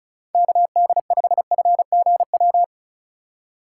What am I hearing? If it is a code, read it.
Morse code, heard as a keyed tone, about 35 wpm, 700 Hz. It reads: KD5FGW
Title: USA Call Signs - 6 Characters - 3x Character Spacing - 35wpm